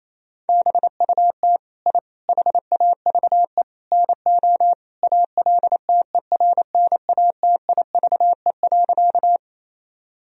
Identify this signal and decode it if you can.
Morse code, heard as a keyed tone, about 28 wpm, 700 Hz. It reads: BUT I HAVE NO ALTERNATIVE.
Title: Sentences from Top 1000 Words - Review Part 4 - 28wpm